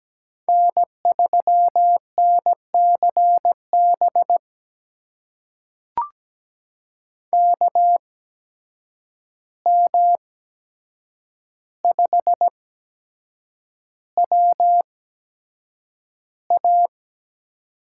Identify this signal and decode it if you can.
Morse code, heard as a keyed tone, about 17 wpm, spaced as for 7 wpm, 700 Hz. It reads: N3NCB KM5WA
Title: USA Call Signs - 5 Characters - 8x Character Spacing - 17wpm